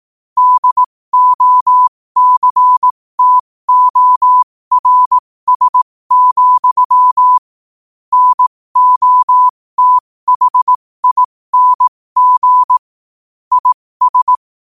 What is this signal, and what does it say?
Morse code, heard as a keyed tone, about 18 wpm, spaced as for 15 wpm, 1000 Hz. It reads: DOCTORS, NOTHING IS
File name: qotdfiles / 15wpm